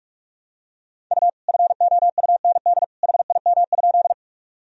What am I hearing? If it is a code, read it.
Morse code, heard as a keyed tone, about 46 wpm, 700 Hz. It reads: U FOUND HIM?
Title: Sentences from Top 300 Words and Abbreviations 45wpm